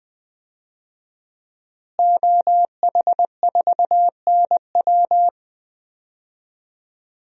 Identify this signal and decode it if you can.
Morse code, heard as a keyed tone, about 21 wpm, 700 Hz. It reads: OH4NW
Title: DXCC-100 Call Signs - 5 Characters - Repeated 3x - 20wpm